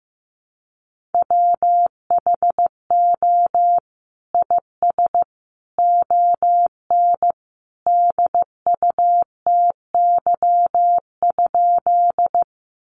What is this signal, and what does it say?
Morse code, heard as a keyed tone, about 15 wpm, 700 Hz. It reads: WHO IS ON DUTY?
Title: Sentences from Top 1500 Words 15wpm